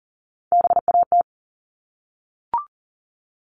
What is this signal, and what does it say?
Morse code, heard as a keyed tone, about 40 wpm, 700 Hz. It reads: BAT
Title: Top 1251-1500 Words 40wpm